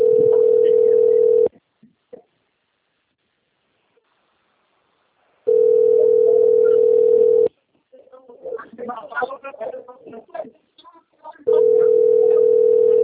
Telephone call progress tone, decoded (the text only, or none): ringback tone